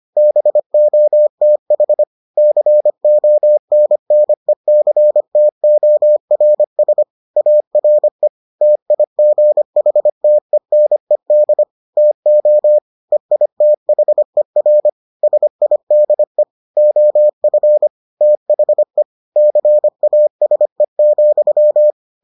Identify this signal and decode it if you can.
Morse code, heard as a keyed tone, about 25 wpm, 600 Hz. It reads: BOTH CONNECTORS ARE TIGHTENED TO EITHER SIDE OF THE CASE,